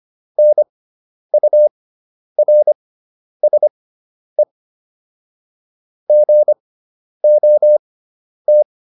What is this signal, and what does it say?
Morse code, heard as a keyed tone, about 25 wpm, spaced as for 10 wpm, 600 Hz. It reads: NURSE GOT